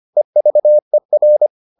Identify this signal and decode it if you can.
Morse code, heard as a keyed tone, about 25 wpm, 600 Hz. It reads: EVER